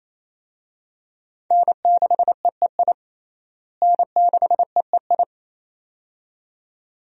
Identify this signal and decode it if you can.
Morse code, heard as a keyed tone, about 28 wpm, 700 Hz. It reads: N6EEI N6EEI
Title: USA Call Signs - 5 Characters - Repeated 2x - 28wpm